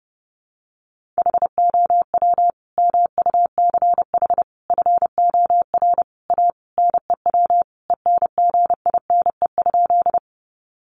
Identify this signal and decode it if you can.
Morse code, heard as a keyed tone, about 30 wpm, 700 Hz. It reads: HOW MUCH FOR A NEW ENGINE?
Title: Sentences from Top 1500 Words 30wpm